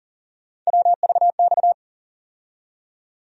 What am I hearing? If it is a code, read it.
Morse code, heard as a keyed tone, about 40 wpm, 700 Hz. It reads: WVX
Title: USA Call Sign Postfixes - 3 Letters 40wpm